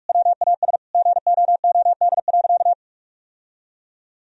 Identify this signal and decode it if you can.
Morse code, heard as a keyed tone, about 46 wpm, 700 Hz. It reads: WAS GOOD.
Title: Sentences from Top 1000 Words - Review Part 3 - 45wpm